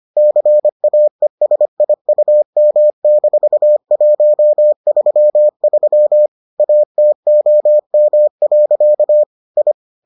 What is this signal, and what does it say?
Morse code, heard as a keyed tone, about 25 wpm, 600 Hz. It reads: CAESIUM-133 ATOM. I